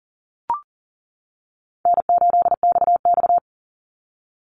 Morse code, heard as a keyed tone, about 40 wpm, 700 Hz. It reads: N8XX
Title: Easy Call Signs - 40wpm